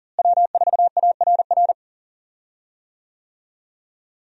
Morse code, heard as a keyed tone, about 40 wpm, 700 Hz. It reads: W4ARR
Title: USA Call Signs - 5 Characters - Repeated 3x - 40wpm